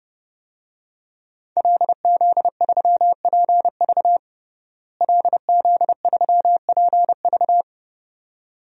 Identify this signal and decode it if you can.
Morse code, heard as a keyed tone, about 30 wpm, 700 Hz. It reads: LZ3PV LZ3PV
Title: DXCC-100 Call Signs - 5 Characters - Repeated 2x - 30wpm